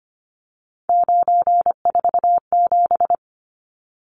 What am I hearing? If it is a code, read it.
Morse code, heard as a keyed tone, about 25 wpm, 700 Hz. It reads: 947
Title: Numbers - 3-Digit 25wpm